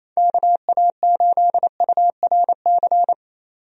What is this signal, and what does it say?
Morse code, heard as a keyed tone, about 28 wpm, 700 Hz. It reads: KA8URC
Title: USA Call Signs - 6 Characters - 28wpm